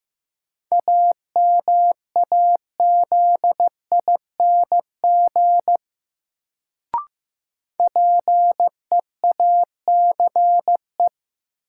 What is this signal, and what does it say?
Morse code, heard as a keyed tone, about 15 wpm, 700 Hz. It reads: AMAZING PEACE
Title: Sentences from Top 1500 Words - New Word Drill Alt - 15wpm